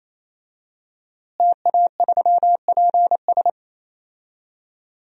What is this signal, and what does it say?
Morse code, heard as a keyed tone, about 28 wpm, 700 Hz. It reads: TA3PS
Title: DXCC-100 Call Signs - 5 Characters - Repeated 3x - 28wpm